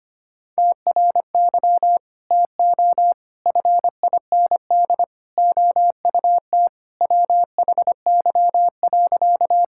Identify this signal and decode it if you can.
Morse code, heard as a keyed tone, about 25 wpm, 700 Hz. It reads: TRY TO FIND OUT WHY.
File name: practice / 25wpm / Sentences with 3-Word Phrases - 25wpm